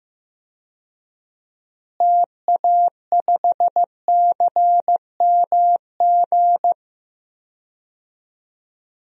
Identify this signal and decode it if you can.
Morse code, heard as a keyed tone, about 16 wpm, 700 Hz. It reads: TA5CMG
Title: DXCC-100 Call Signs - 6 Characters - Repeated 2x - 15wpm